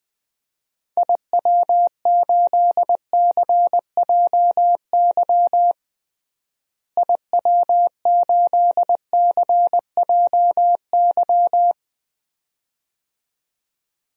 Morse code, heard as a keyed tone, about 20 wpm, 700 Hz. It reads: IW8CJY IW8CJY
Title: DXCC-100 Call Signs - 6 Characters - Repeated 2x - 20wpm